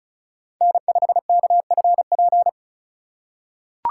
Morse code, heard as a keyed tone, about 34 wpm, 700 Hz. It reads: N5KFP
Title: USA Call Signs - 5 Characters - 2x Character Spacing - 35wpm